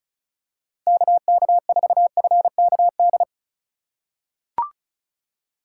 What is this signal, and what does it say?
Morse code, heard as a keyed tone, about 34 wpm, 700 Hz. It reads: KK4FKD
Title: USA Call Signs - 6 Characters - 2x Character Spacing - 35wpm